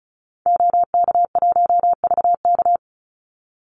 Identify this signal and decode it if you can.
Morse code, heard as a keyed tone, about 34 wpm, 700 Hz. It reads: OK1VK
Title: Call Signs - 35wpm